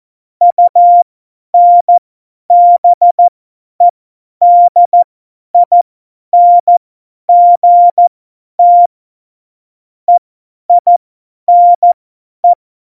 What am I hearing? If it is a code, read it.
Morse code, heard as a keyed tone, about 14 wpm, spaced as for 10 wpm, 700 Hz. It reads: UNBEDINGT EINE
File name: Morsekurs / Witze / Witz 13 March 2018, 50 F70